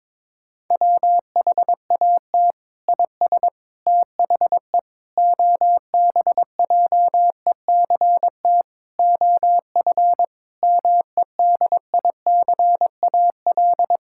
Morse code, heard as a keyed tone, about 22 wpm, 700 Hz. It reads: WHAT IS THE OBJECT OF MEDICAL
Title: Sentences from Top 1000 Words - Review Part 6 - 22wpm